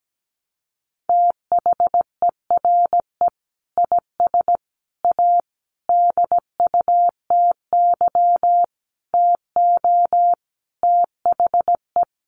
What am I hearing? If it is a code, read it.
Morse code, heard as a keyed tone, about 17 wpm, 700 Hz. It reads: THERE IS A DUTY TO THE
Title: Sentences from Top 1500 Words 17wpm